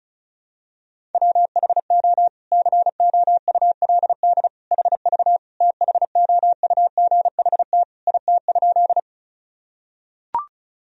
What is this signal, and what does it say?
Morse code, heard as a keyed tone, about 36 wpm, 700 Hz. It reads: WHO COULD HV THOUGHT IT?
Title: Sentences from Top 300 Words and Abbreviations 35wpm